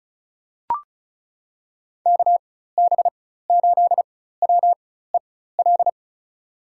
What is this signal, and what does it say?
Morse code, heard as a keyed tone, about 36 wpm, spaced as for 16 wpm, 700 Hz. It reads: KB8WEL
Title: USA Call Signs - 6 Characters - 4x Character Spacing - 35wpm